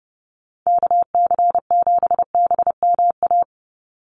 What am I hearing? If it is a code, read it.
Morse code, heard as a keyed tone, about 30 wpm, 700 Hz. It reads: KC7BMA
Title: USA Call Signs 30wpm